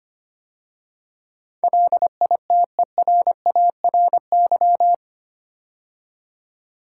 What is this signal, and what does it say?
Morse code, heard as a keyed tone, about 25 wpm, 700 Hz. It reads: LITERARY
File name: 8 Letter Words - First 3 Letters Given - 25wpm